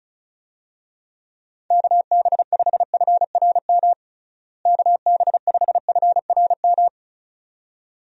Morse code, heard as a keyed tone, about 34 wpm, 700 Hz. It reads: KB5FRM KB5FRM
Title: USA Call Signs - 6 Characters - Repeated 2x - 35wpm